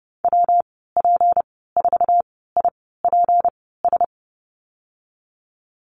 Morse code, heard as a keyed tone, about 30 wpm, spaced as for 17 wpm, 700 Hz. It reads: WP4IPS
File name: Lesson 37g - Letter x - Call Signs 30wpm